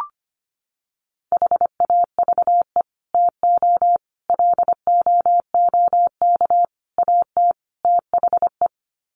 Morse code, heard as a keyed tone, about 25 wpm, 700 Hz. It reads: HAVE TO LOOK AT THE
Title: Sets of 5 Words Encore 25wpm